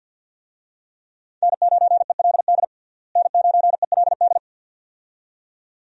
Morse code, heard as a keyed tone, about 48 wpm, 700 Hz. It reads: N9ELD N9ELD
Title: USA Call Signs - 5 Characters - Repeated 2x - 50wpm